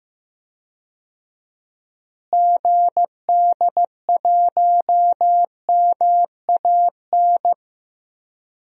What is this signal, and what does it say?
Morse code, heard as a keyed tone, about 15 wpm, 700 Hz. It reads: GD1MAN
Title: DXCC-100 Call Signs - 6 Characters - Repeated 3x - 15wpm